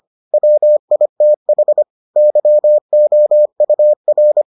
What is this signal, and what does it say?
Morse code, heard as a keyed tone, about 25 wpm, 600 Hz. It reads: WITH YOUR